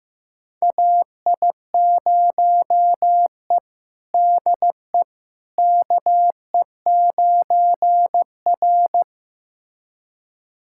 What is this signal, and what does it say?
Morse code, heard as a keyed tone, about 15 wpm, 700 Hz. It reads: AI0E DE KE9R
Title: USA Callsign DE Callsign - 4 Characters - 15wpm